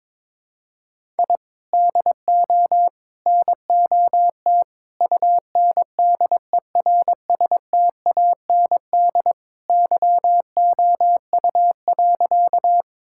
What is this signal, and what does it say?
Morse code, heard as a keyed tone, about 22 wpm, 700 Hz. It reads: I DO NOT UNDERSTAND YOU.
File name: practice / 22wpm / Sentences with 4-Word Phrases - 22wpm